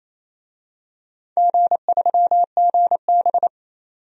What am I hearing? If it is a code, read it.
Morse code, heard as a keyed tone, about 28 wpm, 700 Hz. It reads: G3GB
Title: DXCC-100 Call Signs - 4 Characters - 28wpm